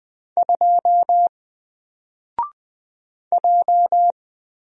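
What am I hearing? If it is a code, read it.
Morse code, heard as a keyed tone, about 20 wpm, 700 Hz. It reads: 2 J
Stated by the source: Single letter and number Morse drill at 20 wpm, from N4LCD.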